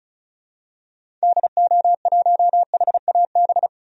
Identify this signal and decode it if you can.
Morse code, heard as a keyed tone, about 36 wpm, 700 Hz. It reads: DO1HAB
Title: DXCC-100 Call Signs - 6 Characters - 35wpm